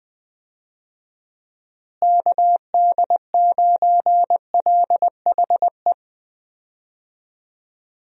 Morse code, heard as a keyed tone, about 20 wpm, 700 Hz. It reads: KD9LHE
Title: USA Call Signs - 6 Characters - Repeated 2x - 20wpm